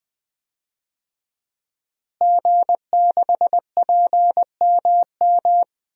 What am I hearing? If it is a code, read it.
Morse code, heard as a keyed tone, about 20 wpm, 700 Hz. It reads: G6PMM
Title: DXCC-100 Call Signs - 5 Characters - Repeated 2x - 20wpm